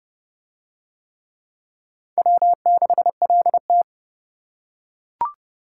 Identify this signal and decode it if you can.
Morse code, heard as a keyed tone, about 28 wpm, 700 Hz. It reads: W6LT